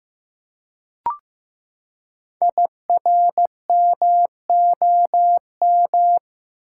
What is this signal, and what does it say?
Morse code, heard as a keyed tone, about 15 wpm, 700 Hz. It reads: IRMOM
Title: Code Groups - 5 Letters - 15wpm